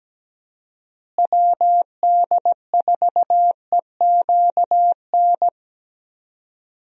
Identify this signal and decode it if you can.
Morse code, heard as a keyed tone, about 17 wpm, 700 Hz. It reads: WD4EQN
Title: USA Call Signs - 6 Characters - Repeated 3x - 17wpm